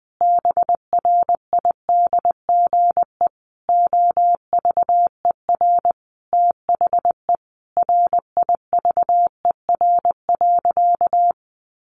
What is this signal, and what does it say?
Morse code, heard as a keyed tone, about 20 wpm, 700 Hz. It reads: BRIDGE OVER THE RIVER.
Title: Sentences from Top 1500 Words 20wpm